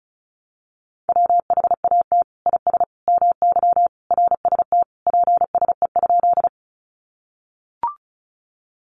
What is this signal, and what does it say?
Morse code, heard as a keyed tone, about 34 wpm, 700 Hz. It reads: WHAT IS MY RST PSE?
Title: Common QSO Phrases 35wpm